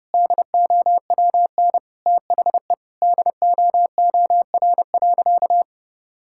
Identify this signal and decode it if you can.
Morse code, heard as a keyed tone, about 30 wpm, 700 Hz. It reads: DOWN THE DOOR.
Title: Sentences from Top 1000 Words - Review Part 6 - 30wpm